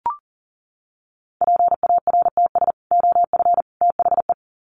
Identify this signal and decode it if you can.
Morse code, heard as a keyed tone, about 40 wpm, 700 Hz. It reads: PARTS OF THE
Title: Sets of 3 Words 40wpm